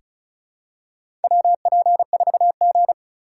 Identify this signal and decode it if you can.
Morse code, heard as a keyed tone, about 34 wpm, 700 Hz. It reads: WP4G
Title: USA Call Signs - 4 Characters - 4x Character Spacing - 35wpm